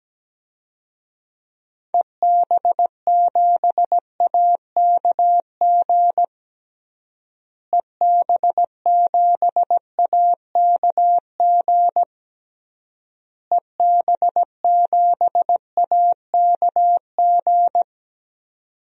Morse code, heard as a keyed tone, about 17 wpm, 700 Hz. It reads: EB7AKG EB7AKG EB7AKG
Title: DXCC-100 Call Signs - 6 Characters - Repeated 3x - 17wpm